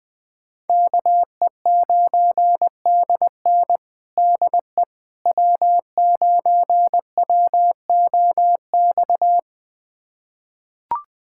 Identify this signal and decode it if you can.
Morse code, heard as a keyed tone, about 21 wpm, 700 Hz. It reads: KE9DN DE W9WOX
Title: USA Callsign DE Callsign - 5 Characters - 20wpm